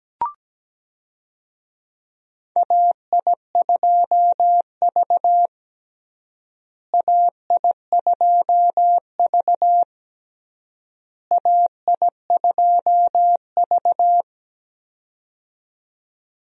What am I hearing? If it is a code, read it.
Morse code, heard as a keyed tone, about 17 wpm, 700 Hz. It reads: AI2V AI2V AI2V
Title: USA Call Signs - 4 Characters - Repeated 3x - 17wpm